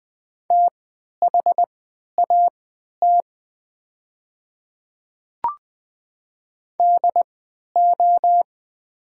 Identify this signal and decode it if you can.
Morse code, heard as a keyed tone, about 20 wpm, spaced as for 11 wpm, 700 Hz. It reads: THAT DO
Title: 3x Character Spacing - Top 100 Words - 20wpm